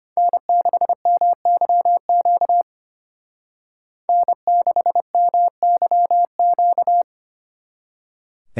Morse code, heard as a keyed tone, about 32 wpm, 700 Hz. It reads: N6MYQ N6MYQ